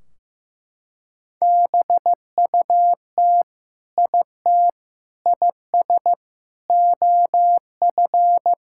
Morse code, heard as a keyed tone, about 16 wpm, 700 Hz. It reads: BUT IT IS OF
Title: Sentences from Top 1600 Words 15wpm